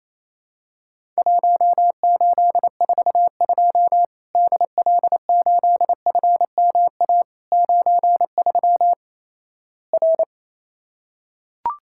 Morse code, heard as a keyed tone, about 28 wpm, 700 Hz. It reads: 1842 DL8FMA 93 R
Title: Worked All Europe DX Contest - QTC Exchange - 28wpm